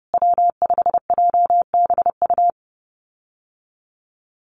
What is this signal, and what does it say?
Morse code, heard as a keyed tone, about 30 wpm, 700 Hz. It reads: W5JBU
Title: USA Call Signs Repeated 3x 30wpm